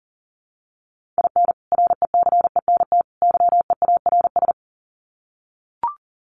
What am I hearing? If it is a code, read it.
Morse code, heard as a keyed tone, about 40 wpm, 700 Hz. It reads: IN RECENT YEARS
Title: Sets of 3 Words 40wpm